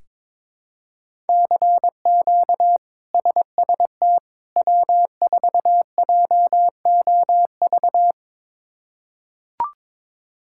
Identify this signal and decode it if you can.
Morse code, heard as a keyed tone, about 22 wpm, 700 Hz. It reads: CQ SST W4JOV
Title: K1USN SST QSO Practice - 22wpm